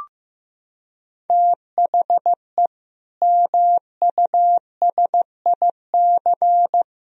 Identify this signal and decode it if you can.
Morse code, heard as a keyed tone, about 15 wpm, 700 Hz. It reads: THE MUSIC
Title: Sentences from Top 1000 Words - Review Part 1 - 15wpm